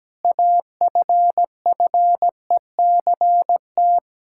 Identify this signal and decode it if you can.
Morse code, heard as a keyed tone, about 17 wpm, 700 Hz. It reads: AFFECT
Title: Sentences from Top 1000 Words - Review Part 6 - 17wpm